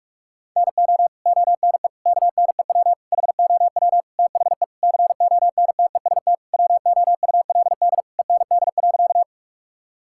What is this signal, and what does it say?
Morse code, heard as a keyed tone, about 46 wpm, 700 Hz. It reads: NO ONE KNEW HOW THE CONTEST WOULD END.
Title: Sentences from Top 1600 Words 45wpm